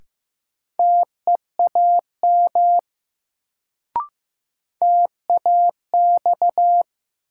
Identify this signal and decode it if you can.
Morse code, heard as a keyed tone, about 15 wpm, 700 Hz. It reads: TEAM TAX
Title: Sentences from Top 1500 Words - New Word Drill Alt - 15wpm